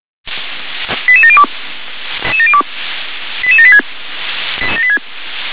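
Five-tone selective call (ZVEI), heard as quick runs of five tones, tones of about 70 ms. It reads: 88702 88786
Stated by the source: Five-tone ZVEI selective call received on SDR.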